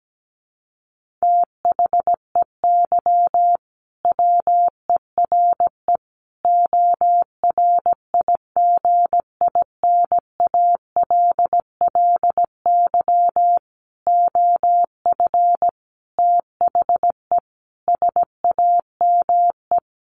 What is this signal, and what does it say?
Morse code, heard as a keyed tone, about 17 wpm, 700 Hz. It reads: THEY WERE ORIGINALLY OF THE SAME
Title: Sentences from Top 1500 Words 17wpm